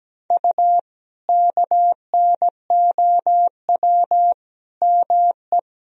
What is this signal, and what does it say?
Morse code, heard as a keyed tone, about 17 wpm, 700 Hz. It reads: U KNOW ME
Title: Sentences from Top 400 Words and Abbreviations 17wpm